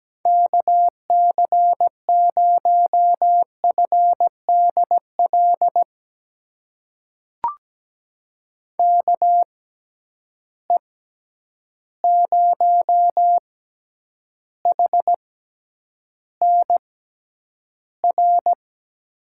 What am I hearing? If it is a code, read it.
Morse code, heard as a keyed tone, about 17 wpm, spaced as for 9 wpm, 700 Hz. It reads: KC0FDL KE0HNR
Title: USA Call Signs - 6 Characters - 6x Character Spacing - 17wpm